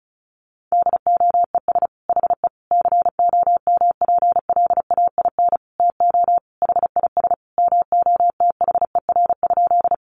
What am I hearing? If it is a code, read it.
Morse code, heard as a keyed tone, about 34 wpm, 700 Hz. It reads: DOES HE COMPLAIN TO HIS MOTHER?
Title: Sentences from Top 1300 Words 35wpm